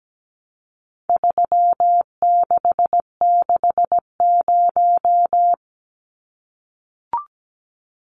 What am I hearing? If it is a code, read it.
Morse code, heard as a keyed tone, about 17 wpm, 700 Hz. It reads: 3660
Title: Numbers - 4-Digit 17wpm